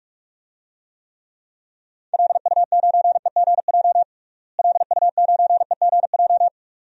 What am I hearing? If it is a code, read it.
Morse code, heard as a keyed tone, about 46 wpm, 700 Hz. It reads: LU9EGJ LU9EGJ
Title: DXCC-100 Call Signs - 6 Characters - Repeated 2x - 45wpm